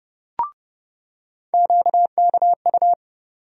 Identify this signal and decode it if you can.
Morse code, heard as a keyed tone, about 30 wpm, 700 Hz. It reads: QKU
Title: USA Call Sign Postfixes - 3 Letters 30wpm